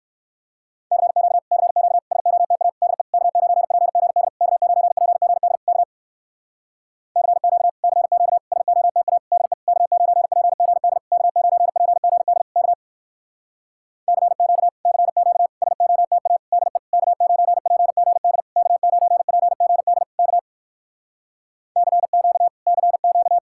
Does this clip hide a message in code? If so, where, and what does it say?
Yes: Morse code from 0.9 to 23.4 s, heard as a keyed tone, about 59 wpm, 700 Hz. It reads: CQ CQ SOTA DE K9PGD K9PGD K CQ CQ SOTA DE K9PGD K9PGD K CQ CQ SOTA DE K9PGD K9PGD K CQ CQ